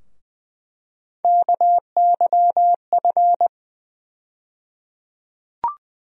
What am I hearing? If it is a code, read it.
Morse code, heard as a keyed tone, about 20 wpm, 700 Hz. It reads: KYF
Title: USA Call Sign Postfixes - 3 Letters 20wpm